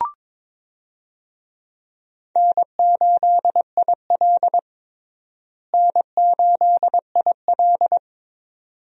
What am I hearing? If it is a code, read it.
Morse code, heard as a keyed tone, about 22 wpm, 700 Hz. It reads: N8IL N8IL
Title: USA Call Signs - 4 Characters - Repeated 2x - 22wpm